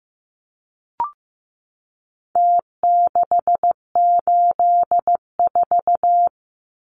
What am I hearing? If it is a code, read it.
Morse code, heard as a keyed tone, about 15 wpm, 700 Hz. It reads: T684
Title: Numbers - Contest Style - 4-Digit 15wpm